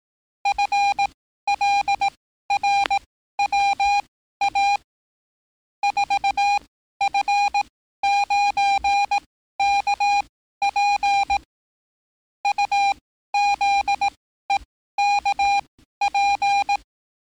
Morse code, heard as a keyed tone, about 18 wpm, spaced as for 13 wpm, 800 Hz. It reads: FLRWA 4F9KP UZEKP
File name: training / NAVPERS 11049to11060 / NAVPERS 11054 RA RB